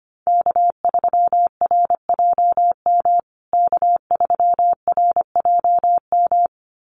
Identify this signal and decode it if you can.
Morse code, heard as a keyed tone, about 25 wpm, 700 Hz. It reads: K3RJM K3RJM
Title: USA Call Signs Repeated 2x 25wpm